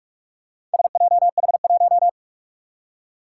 Morse code, heard as a keyed tone, about 44 wpm, 700 Hz. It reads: SJH1
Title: Four Challenging Letter-Number Combinations 45wpm